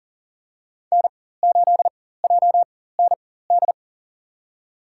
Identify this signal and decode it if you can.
Morse code, heard as a keyed tone, about 40 wpm, spaced as for 19 wpm, 700 Hz. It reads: N8JND
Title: USA Call Signs - 4x Character Spacing - 40wpm